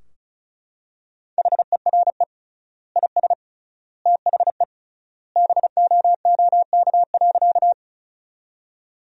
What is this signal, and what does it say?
Morse code, heard as a keyed tone, about 36 wpm, 700 Hz. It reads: HERE IS THE BOOK.
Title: 3x Word-Spacing - Sentences from Top 300 Words - 35wpm